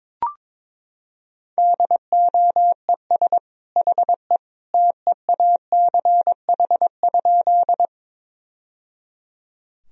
Morse code, heard as a keyed tone, about 22 wpm, 700 Hz. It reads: DOES HE TEACH?
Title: Sentences from Top 1000 Words - Review Part 2 - 22wpm